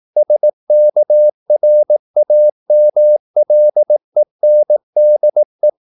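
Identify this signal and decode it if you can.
Morse code, heard as a keyed tone, about 18 wpm, 600 Hz. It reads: SKRAMLENDE
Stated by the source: Slow (QRS) Morse practice at 18 wpm, CW med Gnister.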